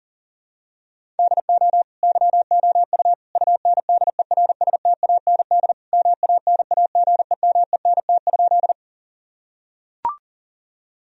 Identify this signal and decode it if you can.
Morse code, heard as a keyed tone, about 40 wpm, 700 Hz. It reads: DO YOU UNDERSTAND MANAGEMENT?
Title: Sentences from Top 1000 Words - Review Part 4 - 40wpm